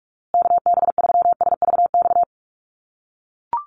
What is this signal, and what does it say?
Morse code, heard as a keyed tone, about 44 wpm, 700 Hz. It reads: KB3SVX
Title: USA Call Signs Repeated 2x 45wpm